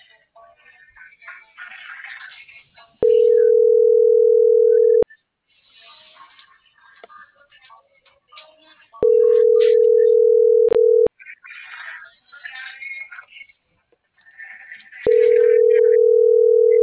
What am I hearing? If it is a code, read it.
Ringback tone (North America), 440 + 480 Hz.